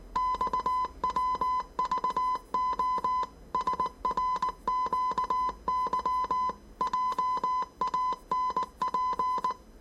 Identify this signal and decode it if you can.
Morse code, heard as a keyed tone, about 19 wpm, 1010 Hz. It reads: XWVO SRQY JANP
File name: 业余无线电 / ●未分类 / 字码80